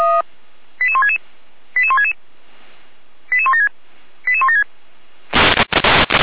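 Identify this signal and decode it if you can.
Five-tone selective call (ZVEI), heard as quick runs of five tones, tones of about 80 ms. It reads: 88166 88166 88176 88176
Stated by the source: ZVEI five-tone selective call received on SDR, address 88090.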